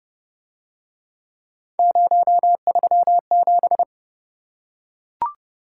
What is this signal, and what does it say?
Morse code, heard as a keyed tone, about 29 wpm, 700 Hz. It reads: 037